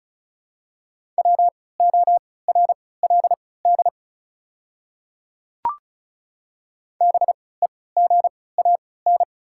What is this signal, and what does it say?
Morse code, heard as a keyed tone, about 36 wpm, spaced as for 20 wpm, 690 Hz. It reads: WORLD BEGAN
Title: 3x Character Spacing - Top 300 Words - 35wpm